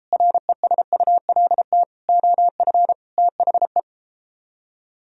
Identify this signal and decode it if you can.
Morse code, heard as a keyed tone, about 33 wpm, 700 Hz. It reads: RESULT OF THE